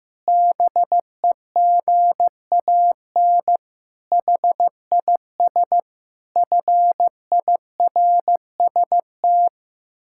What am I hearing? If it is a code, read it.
Morse code, heard as a keyed tone, about 15 wpm, 700 Hz. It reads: BEGAN HIS FIRST
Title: Pangram Challenge v2 - 15wpm